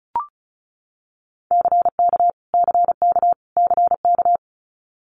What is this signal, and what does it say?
Morse code, heard as a keyed tone, about 36 wpm, 700 Hz. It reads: CK CK CK
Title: Multi-letter Phonemes Repeated 3x 35wpm